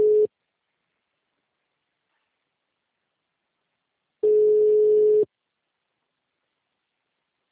Ringback tone (Europe, 425 Hz).